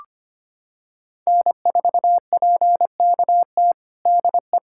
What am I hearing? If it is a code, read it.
Morse code, heard as a keyed tone, about 25 wpm, 700 Hz. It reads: N4PKT DE
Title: USA Callsign DE Callsign - 5 Characters - 25wpm